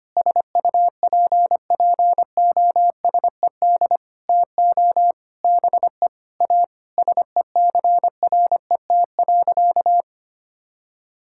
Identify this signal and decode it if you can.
Morse code, heard as a keyed tone, about 25 wpm, 700 Hz. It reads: SUPPOSED TO BE A SECRET.
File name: practice / 25wpm / Sentences with 4-Word Phrases - 25wpm